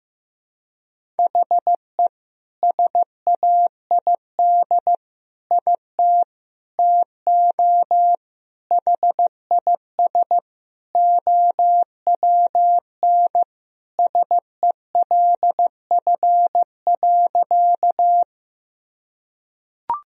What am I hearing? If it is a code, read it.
Morse code, heard as a keyed tone, about 15 wpm, 700 Hz. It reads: HE SAID IT TO HIS OWN SELF.
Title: Sentences from Top 1000 Words - Review Part 1 - 15wpm